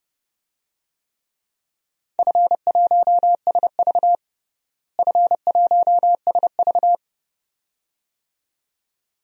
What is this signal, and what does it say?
Morse code, heard as a keyed tone, about 32 wpm, 700 Hz. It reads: F1SV F1SV
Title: DXCC-100 Call Signs - 4 Characters - Repeated 2x - 30wpm